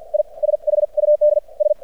Morse code, heard as a keyed tone, about 48 wpm, spaced as for 41 wpm, 600 Hz. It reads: I SHUD S
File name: archived / CRSnet